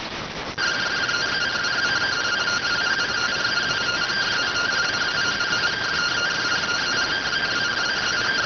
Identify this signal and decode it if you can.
Two-tone teleprinter signal, about 170 Hz shift.